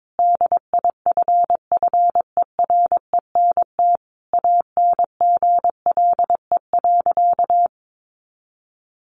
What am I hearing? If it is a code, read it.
Morse code, heard as a keyed tone, about 22 wpm, 700 Hz. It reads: DIFFERENT ANGLE.
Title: Sentences from Top 1500 Words 22wpm